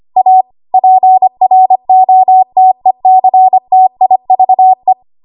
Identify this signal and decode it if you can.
Morse code, heard as a keyed tone, about 25 wpm, 750 Hz. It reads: A PROTECTIVE
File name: Archive / 25 WPM / 25WPM